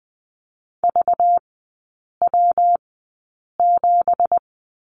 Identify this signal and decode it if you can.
Morse code, heard as a keyed tone, about 20 wpm, spaced as for 8 wpm, 700 Hz. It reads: VW7
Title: 198 Lesson 32e - Letter k - Three Characters 20wpm